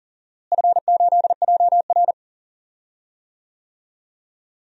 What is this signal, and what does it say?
Morse code, heard as a keyed tone, about 40 wpm, 700 Hz. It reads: F8JR
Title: DXCC-100 Call Signs - 4 Characters - Repeated 3x - 40wpm